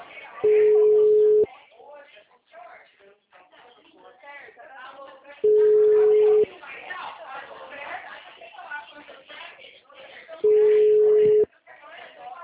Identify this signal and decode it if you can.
Ringback tone (Europe, 425 Hz), with speech.